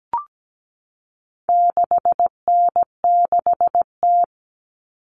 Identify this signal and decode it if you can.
Morse code, heard as a keyed tone, about 17 wpm, 700 Hz. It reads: 6N6T
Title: All Cut Numbers - 4-Digit 17wpm